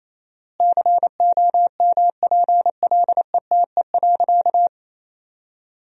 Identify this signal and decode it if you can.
Morse code, heard as a keyed tone, about 28 wpm, 700 Hz. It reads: COMPLETE.
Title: 4x Word-Spacing - Sentences from Top 400 Words - 28wpm